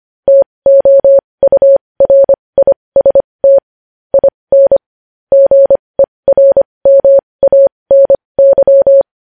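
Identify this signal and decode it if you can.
Morse code, heard as a keyed tone, about 25 wpm, spaced as for 20 wpm, 550 Hz. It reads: TOURIST IN GERMANY
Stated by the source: Morse news headlines sent at 20.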